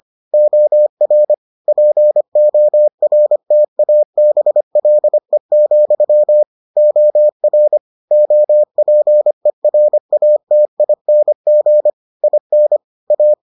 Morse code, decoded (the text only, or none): OR PORTABLE, OR OPERATING IN A